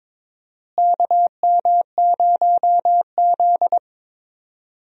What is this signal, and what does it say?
Morse code, heard as a keyed tone, about 22 wpm, 700 Hz. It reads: KM0Z
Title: USA Call Signs - 4 Characters - Repeated 3x - 22wpm